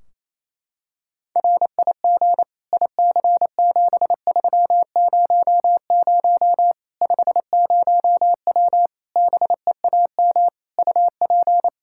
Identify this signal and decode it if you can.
Morse code, heard as a keyed tone, about 28 wpm, 700 Hz. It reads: RIG IC7300 50W BEAM UP
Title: Practice QSO - Exchange 2 - Info - 28wpm